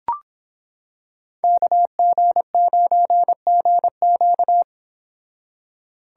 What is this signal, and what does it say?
Morse code, heard as a keyed tone, about 26 wpm, 700 Hz. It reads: KG9GQ